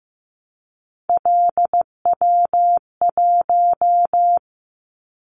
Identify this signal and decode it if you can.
Morse code, heard as a keyed tone, about 15 wpm, 700 Hz. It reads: LW1
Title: Callsign Prefixes for Top 100 DXCC - 15wpm